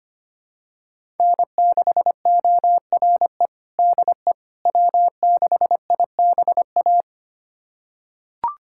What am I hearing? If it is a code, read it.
Morse code, heard as a keyed tone, about 25 wpm, 700 Hz. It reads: N6ORE DE W6IBA
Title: USA Callsign DE Callsign - 5 Characters - 25wpm